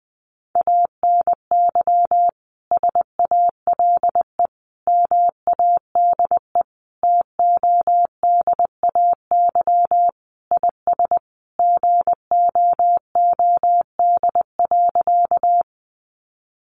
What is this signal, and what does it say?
Morse code, heard as a keyed tone, about 20 wpm, 700 Hz. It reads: ANY SALE MADE TODAY IS GOOD.
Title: Sentences from Top 1400 Words 20wpm